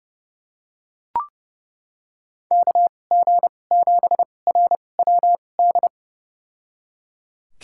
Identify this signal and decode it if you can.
Morse code, heard as a keyed tone, about 30 wpm, spaced as for 22 wpm, 700 Hz. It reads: KG7RWD
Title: USA Call Signs - 6 Characters - 2x Character Spacing - 30wpm